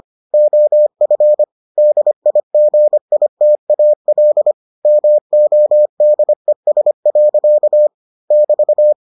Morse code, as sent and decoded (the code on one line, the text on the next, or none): --- ..-. / -.. .. --. .. - .- .-.. / -- --- -.. ... .-.-.- / -...-
OF DIGITAL MODES. <BT>